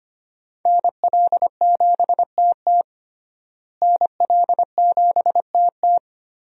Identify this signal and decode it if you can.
Morse code, heard as a keyed tone, about 25 wpm, 700 Hz. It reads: NL7TT NL7TT
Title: USA Call Signs - 5 Characters - Repeated 2x - 25wpm